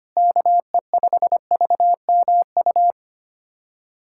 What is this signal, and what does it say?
Morse code, heard as a keyed tone, about 25 wpm, 700 Hz. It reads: KE5VMU
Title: USA Callsign DE Callsign - 6 Characters - 25wpm